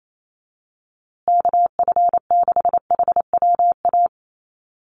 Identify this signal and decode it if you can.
Morse code, heard as a keyed tone, about 28 wpm, 700 Hz. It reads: KF6HWA
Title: USA Call Signs Repeated 2x 28wpm